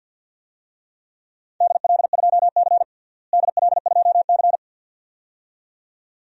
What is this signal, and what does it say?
Morse code, heard as a keyed tone, about 52 wpm, 700 Hz. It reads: DL2C DL2C
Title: DXCC-100 Call Signs - 4 Characters - Repeated 2x - 50wpm